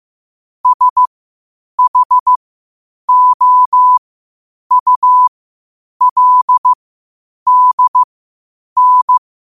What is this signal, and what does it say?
Morse code, heard as a keyed tone, about 15 wpm, spaced as for 8 wpm, 1000 Hz. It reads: SHOULDN
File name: qotdfiles / 8wpm